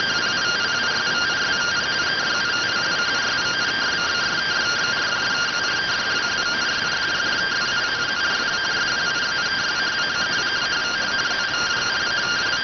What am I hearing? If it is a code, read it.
Two-tone teleprinter signal, about 170 Hz shift.